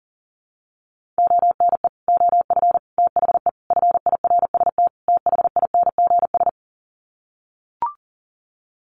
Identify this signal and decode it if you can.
Morse code, heard as a keyed tone, about 40 wpm, 700 Hz. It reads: ONE OF THE FIRST THINGS
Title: Sets of 5 Words Encore 40wpm